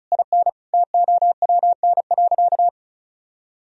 Morse code, heard as a keyed tone, about 34 wpm, 700 Hz. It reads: IN TOWN.
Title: Sentences from Top 1000 Words - Review Part 4 - 35wpm